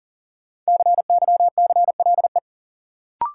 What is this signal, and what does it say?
Morse code, heard as a keyed tone, about 40 wpm, 700 Hz. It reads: CYCLE
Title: Sentences from Top 900 Words - New Word Drill - 40wpm